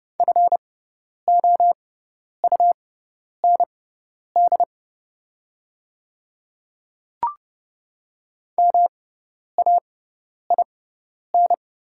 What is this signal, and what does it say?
Morse code, heard as a keyed tone, about 30 wpm, spaced as for 10 wpm, 700 Hz. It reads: FOUND MAIN
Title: 6x Character Spacing - Top 300 Words - 30wpm